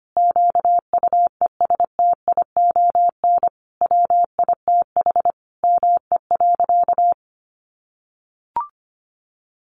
Morse code, heard as a keyed tone, about 25 wpm, 700 Hz. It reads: QUESTION WITH ME.
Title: Sentences from Top 1500 Words 25wpm